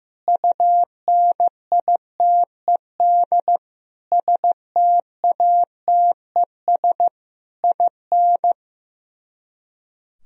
Morse code, heard as a keyed tone, about 15 wpm, 700 Hz. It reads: UNITED STATES IN
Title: Speed-Racing - Sets of 5 Words 15wpm